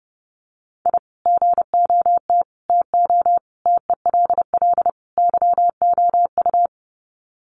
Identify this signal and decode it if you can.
Morse code, heard as a keyed tone, about 30 wpm, 700 Hz. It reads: I GOT TO TELL YOU
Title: Sets of 5 Words Encore2 30wpm